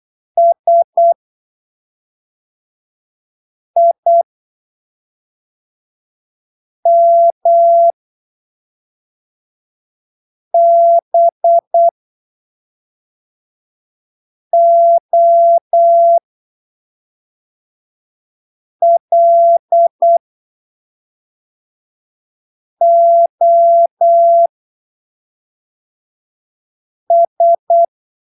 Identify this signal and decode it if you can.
Morse code, heard as a keyed tone, about 8 wpm, spaced as for 3 wpm, 680 Hz. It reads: SIMBOLOS
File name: arquivos / texto / texto 18 08ppm